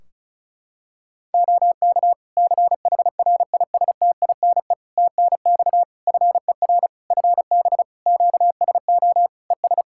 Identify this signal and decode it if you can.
Morse code, heard as a keyed tone, about 36 wpm, 700 Hz. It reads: OK CHRISTINE TNX FER FB QSO ES
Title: Practice QSO - Exchange 4 - Ending - 35wpm